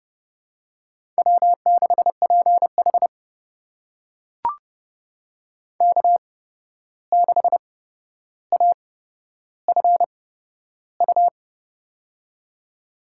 Morse code, heard as a keyed tone, about 30 wpm, spaced as for 12 wpm, 700 Hz. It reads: W6PH K6AFU
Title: USA Call Signs - 8x Character Spacing - 30wpm